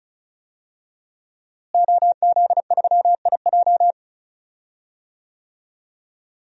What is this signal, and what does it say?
Morse code, heard as a keyed tone, about 36 wpm, 700 Hz. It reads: OZ3IJ
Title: DXCC-100 Call Signs - 5 Characters - Repeated 3x - 35wpm